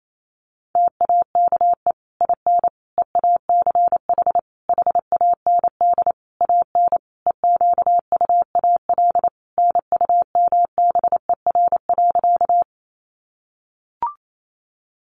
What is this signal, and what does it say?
Morse code, heard as a keyed tone, about 28 wpm, 700 Hz. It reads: TAKE IN EACH HAND AN EQUAL NUMBER.
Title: Sentences from Top 1400 Words 28wpm